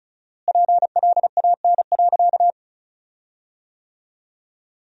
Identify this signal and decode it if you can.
Morse code, heard as a keyed tone, about 36 wpm, 700 Hz. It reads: PLAN.
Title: 4x Word-Spacing - Sentences from Top 500 Words - 35wpm